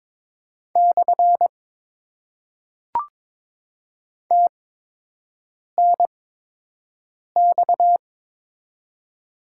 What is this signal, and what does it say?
Morse code, heard as a keyed tone, about 23 wpm, 700 Hz. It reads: / T N X
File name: Top 90 QSO Elements - 8x Character Spacing - 22wpm